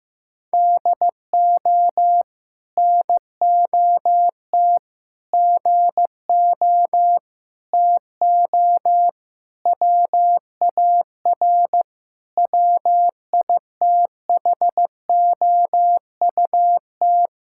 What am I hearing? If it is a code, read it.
Morse code, heard as a keyed tone, about 15 wpm, 700 Hz. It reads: DO NOT GO TO WAR WITHOUT
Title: Sentences from Top 1000 Words - Review Part 5 - 15wpm